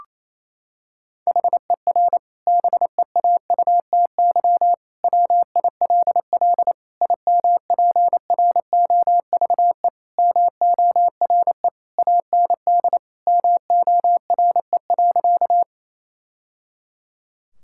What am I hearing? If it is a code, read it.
Morse code, heard as a keyed tone, about 28 wpm, 700 Hz. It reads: HER BEAUTY WILL IMPROVE MORE AND MORE.
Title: Sentences from Top 1000 Words - Review Part 5 - 28wpm